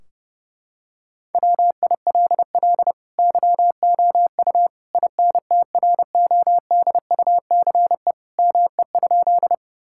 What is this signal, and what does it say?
Morse code, heard as a keyed tone, about 32 wpm, 700 Hz. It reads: WILL YOU INTRODUCE ME?
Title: Sentences from Top 1600 Words 30wpm